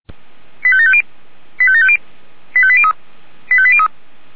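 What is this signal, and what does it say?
Five-tone selective call (ZVEI), heard as quick runs of five tones, tones of about 70 ms. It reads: 85750 85750 85803 85803
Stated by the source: Five-tone ZVEI selective call 89145 received by SDR.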